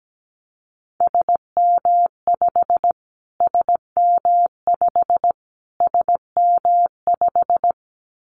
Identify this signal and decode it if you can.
Morse code, heard as a keyed tone, about 17 wpm, 700 Hz. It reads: SM5 SM5 SM5
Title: Callsign Prefixes for Top 25 DXCC - Repeated 3x - 17wpm